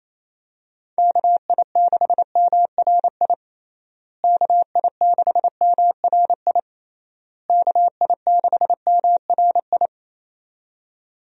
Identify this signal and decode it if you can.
Morse code, heard as a keyed tone, about 28 wpm, 700 Hz. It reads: KI6MRI KI6MRI KI6MRI
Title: USA Call Signs - 6 Characters - Repeated 3x - 28wpm